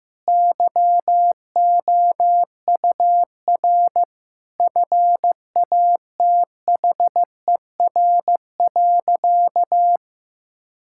Morse code, heard as a keyed tone, about 15 wpm, 700 Hz. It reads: YOUR FATHER.
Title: Sentences from Top 1000 Words - Review Part 6 - 15wpm